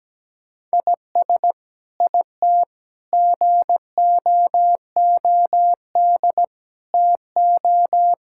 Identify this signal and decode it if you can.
Morse code, heard as a keyed tone, about 17 wpm, 700 Hz. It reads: IS IT GOOD TO
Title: Sentences from Top 1000 Words - Review Part 6 - 17wpm